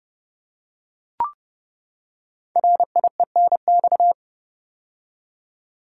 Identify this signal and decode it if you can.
Morse code, heard as a keyed tone, about 30 wpm, 700 Hz. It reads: RIENX
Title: Code Groups - 5 Letters - 30wpm